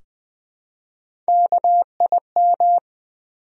Morse code, heard as a keyed tone, about 20 wpm, 700 Hz. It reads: KIM
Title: USA Common Names - Max 4 Letters - 20wpm